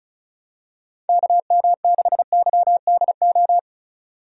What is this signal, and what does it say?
Morse code, heard as a keyed tone, about 34 wpm, 700 Hz. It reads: KM6YDO
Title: USA Call Signs - 6 Characters - 35wpm